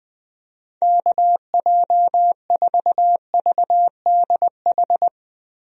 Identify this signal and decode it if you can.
Morse code, heard as a keyed tone, about 20 wpm, 700 Hz. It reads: KJ4VDH
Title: USA Call Signs - 6 Characters - Repeated 2x - 20wpm